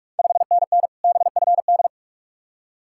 Morse code, heard as a keyed tone, about 44 wpm, 700 Hz. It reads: 5NN BFD
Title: Ontario QSO Party Exchange 45wpm